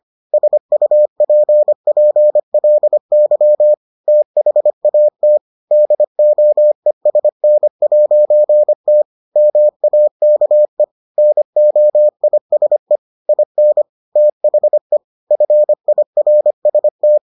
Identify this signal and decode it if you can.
Morse code, heard as a keyed tone, about 25 wpm, 600 Hz. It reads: SUPPLY THAT DOESN'T MAKE NOISE IN THE FIRST